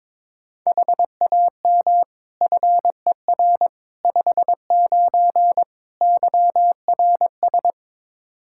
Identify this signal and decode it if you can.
Morse code, heard as a keyed tone, about 21 wpm, 700 Hz. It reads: HAM FER 59 YRS
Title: Practice QSO - Exchange 3 - More Info - 22wpm